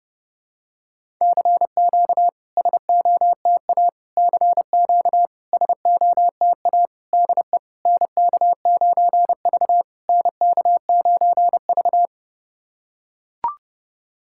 Morse code, heard as a keyed tone, about 30 wpm, 700 Hz. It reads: CQ SOTA CQ SOTA DE NK9V NK9V
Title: SOTA QSO Practice 30wpm